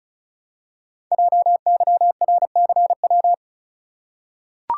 Morse code, heard as a keyed tone, about 36 wpm, 700 Hz. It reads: JYRCW
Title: Code Groups - 5 Letter:Numbers - 35wpm